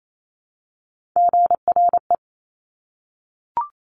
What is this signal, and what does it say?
Morse code, heard as a keyed tone, about 28 wpm, 700 Hz. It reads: GRE
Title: Top Three Letter Combinations in Words 28wpm